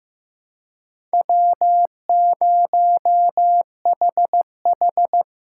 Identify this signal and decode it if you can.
Morse code, heard as a keyed tone, about 15 wpm, 700 Hz. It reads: W0HH
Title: USA Call Signs - 4 Characters - 4x Character Spacing - 15wpm